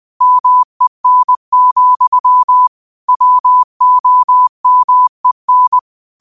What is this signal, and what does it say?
Morse code, heard as a keyed tone, about 20 wpm, 1000 Hz. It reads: MEN, WOMEN
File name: qotdfiles / 20wpm